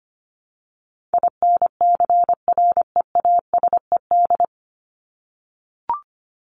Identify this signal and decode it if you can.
Morse code, heard as a keyed tone, about 25 wpm, 700 Hz. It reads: INCREASED
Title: ED Suffix - Top 100 Words with Base Words 25wpm